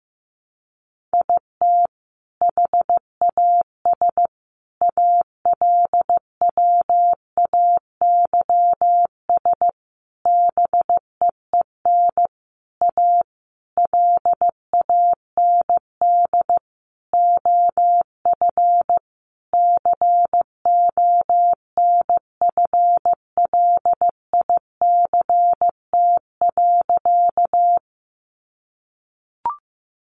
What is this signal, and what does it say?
Morse code, heard as a keyed tone, about 15 wpm, 700 Hz. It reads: IT HAS ALWAYS BEEN A LAND OF CONFLICT.
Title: Sentences from Top 1300 Words 15wpm